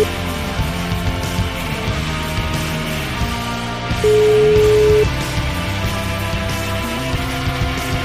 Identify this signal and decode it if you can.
Ringback tone (Europe, 425 Hz), with noise.